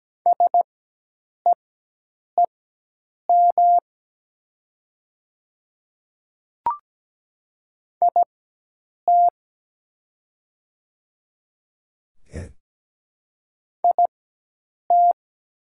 Morse code, heard as a keyed tone, about 17 wpm, spaced as for 8 wpm, 700 Hz. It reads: SEEM IT IT
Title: 4x Character Spacing - Top 300 Words - 17wpm